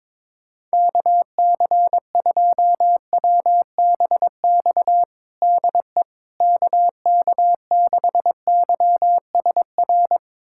Morse code, heard as a keyed tone, about 22 wpm, 700 Hz. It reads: KC2WBX DE KK6YSR
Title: USA Callsign DE Callsign - 6 Characters - 22wpm